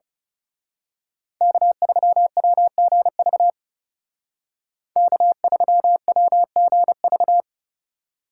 Morse code, heard as a keyed tone, about 31 wpm, 700 Hz. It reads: K3WGV K3WGV
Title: Speed Racing - USA 1x3 Call Signs 30wpm